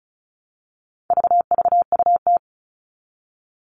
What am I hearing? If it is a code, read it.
Morse code, heard as a keyed tone, about 34 wpm, 700 Hz. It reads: VVUT
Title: All Cut Numbers - 4-Digit 35wpm